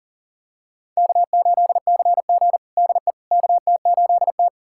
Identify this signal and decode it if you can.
Morse code, heard as a keyed tone, about 37 wpm, 700 Hz. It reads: K8CG DE KT8T
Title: USA Callsign DE Callsign - 4 Characters - 40wpm